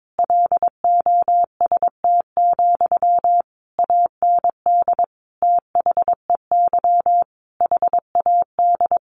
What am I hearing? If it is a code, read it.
Morse code, heard as a keyed tone, about 22 wpm, 700 Hz. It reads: LOST, AND THEY HAD
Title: Sentences from Top 1500 Words 22wpm